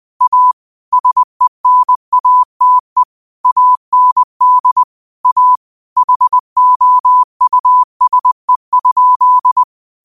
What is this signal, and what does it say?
Morse code, heard as a keyed tone, about 20 wpm, 1000 Hz. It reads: A SENATE AND A HOUSE?
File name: qotdfiles / 20wpm